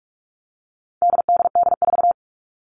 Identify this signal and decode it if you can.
Morse code, heard as a keyed tone, about 44 wpm, 700 Hz. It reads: DDD4
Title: All Cut Numbers - 4-Digit 45wpm